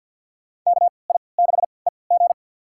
Morse code, heard as a keyed tone, about 51 wpm, spaced as for 28 wpm, 700 Hz. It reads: KI6EG
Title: USA Call Signs - 5 Characters - 3x Character Spacing - 50wpm